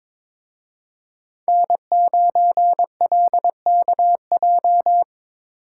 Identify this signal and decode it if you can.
Morse code, heard as a keyed tone, about 26 wpm, 700 Hz. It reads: N9LKJ
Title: Speed Racing - USA 1x3 Call Signs 22wpm